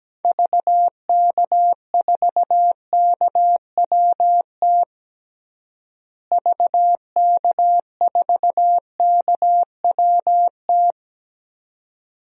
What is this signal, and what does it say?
Morse code, heard as a keyed tone, about 17 wpm, 700 Hz. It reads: VK4KWT VK4KWT
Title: DXCC-100 Call Signs - 6 Characters - Repeated 2x - 17wpm